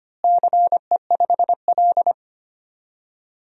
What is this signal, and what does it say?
Morse code, heard as a keyed tone, about 25 wpm, 700 Hz. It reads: CE5L
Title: DXCC-100 Call Signs - 4 Characters - Repeated 3x - 25wpm